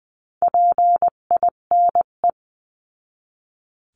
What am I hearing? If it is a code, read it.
Morse code, heard as a keyed tone, about 20 wpm, 700 Hz. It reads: PINE
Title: 258 Lesson 42c - Words 16-20wpm